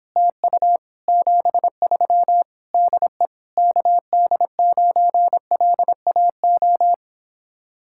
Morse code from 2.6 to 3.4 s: DE

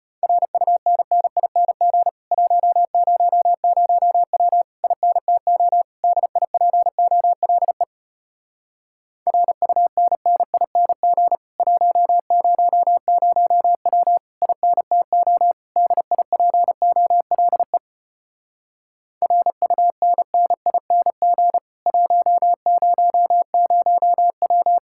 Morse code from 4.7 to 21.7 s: INTO DIPOLE RUNNING 100W INTO DIPOLE RUNNING

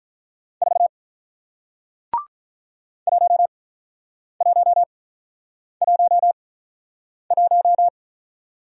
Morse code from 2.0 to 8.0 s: 1111

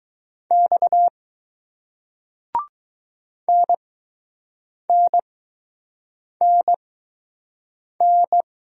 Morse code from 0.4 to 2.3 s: X

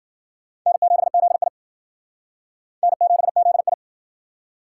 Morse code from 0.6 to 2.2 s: N7ZI

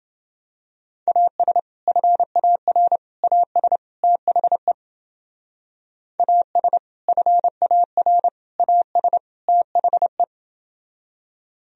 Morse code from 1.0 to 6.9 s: AS FAR AS THE AS

